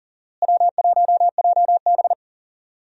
Morse code, heard as a keyed tone, about 40 wpm, 700 Hz. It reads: W1JB
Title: USA Call Signs - 4 Characters - Repeated 2x - 40wpm